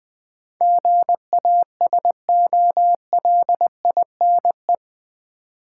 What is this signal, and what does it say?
Morse code, heard as a keyed tone, about 20 wpm, 700 Hz. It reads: GASOLINE
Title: Easy Words with Hints - 20wpm